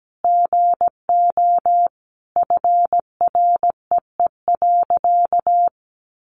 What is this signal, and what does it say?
Morse code, heard as a keyed tone, about 17 wpm, 700 Hz. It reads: GO FREE.
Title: Sentences from Top 500 Words 17wpm